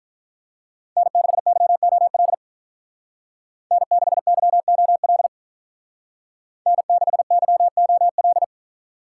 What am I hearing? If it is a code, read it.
Morse code, heard as a keyed tone, about 46 wpm, 700 Hz. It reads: N6YOL N6YOL N6YOL